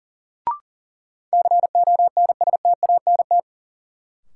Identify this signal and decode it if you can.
Morse code, heard as a keyed tone, about 40 wpm, 700 Hz. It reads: CONSTANT
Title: Sentences from Top 1500 Words - New Word Drill Alt - 40wpm